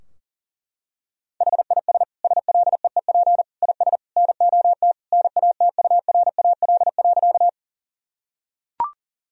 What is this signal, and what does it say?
Morse code, heard as a keyed tone, about 37 wpm, 700 Hz. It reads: HIS SLEEP IS NOT NATURAL.
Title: Sentences from Top 1000 Words - Review Part 4 - 40wpm